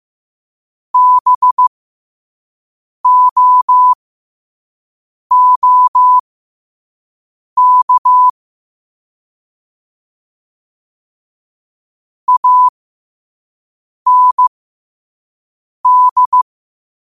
Morse code, heard as a keyed tone, about 15 wpm, spaced as for 5 wpm, 1000 Hz. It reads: BOOK AND
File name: qotdfiles / 5wpm